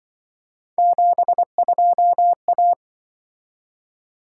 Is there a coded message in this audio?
Yes: Morse code from 0.8 to 2.7 s, heard as a keyed tone, about 25 wpm, 700 Hz. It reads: 72A